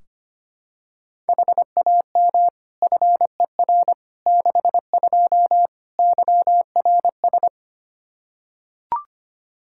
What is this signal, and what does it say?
Morse code, heard as a keyed tone, about 25 wpm, 700 Hz. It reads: HAM FER 62 YRS
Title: K1USN SST QSO Practice 25wpm